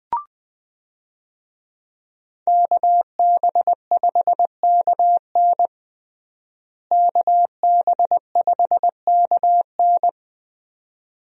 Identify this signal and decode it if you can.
Morse code, heard as a keyed tone, about 20 wpm, 700 Hz. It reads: KB5KN KB5KN
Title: USA Call Signs - 5 Characters - Repeated 2x - 20wpm